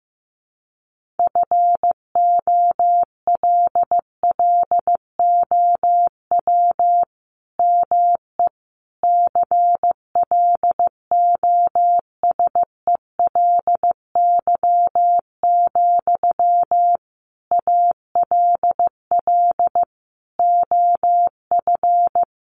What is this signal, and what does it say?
Morse code, heard as a keyed tone, about 15 wpm, 700 Hz. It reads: FOLLOW ME CLOSELY, ALL OF
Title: Sentences from Top 1400 Words 15wpm